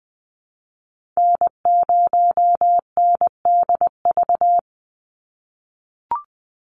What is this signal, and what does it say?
Morse code, heard as a keyed tone, about 20 wpm, 700 Hz. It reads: N0NDV
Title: USA Call Signs 20wpm